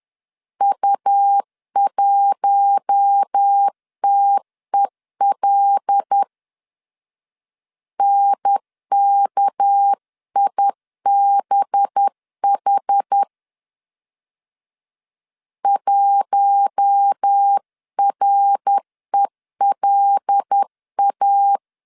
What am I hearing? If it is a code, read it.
Morse code, heard as a keyed tone, about 10 wpm, 800 Hz. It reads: U1TEL NKIBH 1RELA